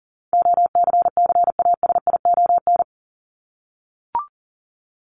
Morse code, heard as a keyed tone, about 40 wpm, 700 Hz. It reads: OCCASION
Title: Top 1501-1750 Words 40wpm